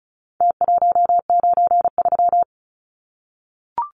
Morse code, heard as a keyed tone, about 34 wpm, 700 Hz. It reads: T193
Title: Numbers - Contest Style - 4-Digit 35wpm